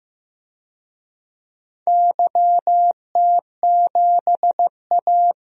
Morse code, heard as a keyed tone, about 15 wpm, 700 Hz. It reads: YT7A
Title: DXCC-100 Call Signs - 4 Characters - Repeated 3x - 15wpm